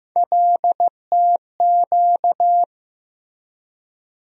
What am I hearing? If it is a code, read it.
Morse code, heard as a keyed tone, about 15 wpm, 700 Hz. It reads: LTQ
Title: USA Call Sign Postfixes - 3 Letters 15wpm